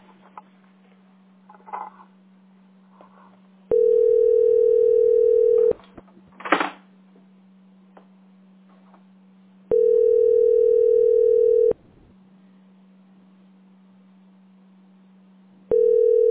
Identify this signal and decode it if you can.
Ringback tone (North America), 440 + 480 Hz.
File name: 17fc2dde-e349-411a-8a15